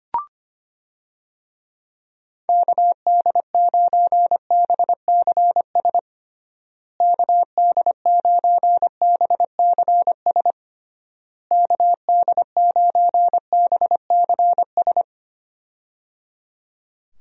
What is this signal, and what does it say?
Morse code, heard as a keyed tone, about 25 wpm, 700 Hz. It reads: KD9BCS KD9BCS KD9BCS
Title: USA Call Signs - 6 Characters - Repeated 3x - 25wpm